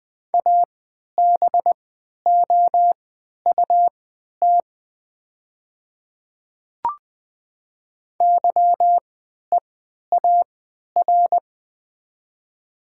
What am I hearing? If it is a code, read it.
Morse code, heard as a keyed tone, about 20 wpm, spaced as for 11 wpm, 700 Hz. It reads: ABOUT YEAR
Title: 3x Character Spacing - Top 200 Words - 20wpm